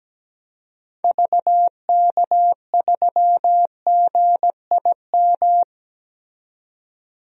Morse code, heard as a keyed tone, about 17 wpm, 700 Hz. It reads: VK3GIM
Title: DXCC-100 Call Signs - 6 Characters - 17wpm